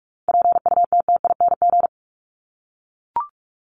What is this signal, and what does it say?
Morse code, heard as a keyed tone, about 44 wpm, 700 Hz. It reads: PUTTING
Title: ING Suffix - Top 100 Words with Base Words 45wpm